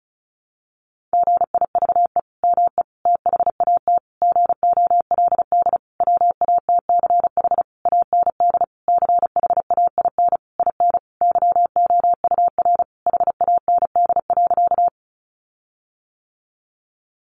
Morse code, heard as a keyed tone, about 34 wpm, 700 Hz. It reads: GIVE ME THAT GOLD WATCH AND CHAIN IN YOUR HAND.
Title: Sentences from Top 1500 Words 35wpm